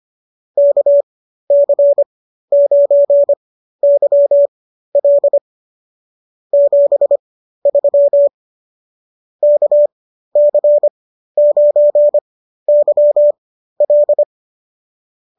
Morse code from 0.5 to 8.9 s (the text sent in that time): KC9YL 73